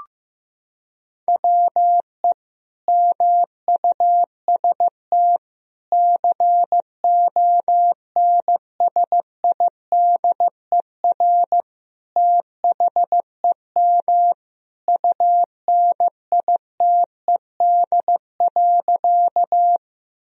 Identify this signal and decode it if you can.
Morse code, heard as a keyed tone, about 15 wpm, 700 Hz. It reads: WE MUST CONSIDER THEM UNITED.
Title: Sentences from Top 1000 Words - Review Part 4 - 15wpm